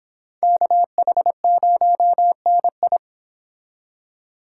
Morse code, heard as a keyed tone, about 26 wpm, 700 Hz. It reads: KH0NI